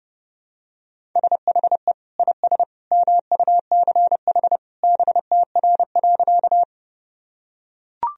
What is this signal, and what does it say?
Morse code, heard as a keyed tone, about 32 wpm, 700 Hz. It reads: SHE IS MUCH BTR.
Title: Sentences from Top 400 Words and Abbreviations 30wpm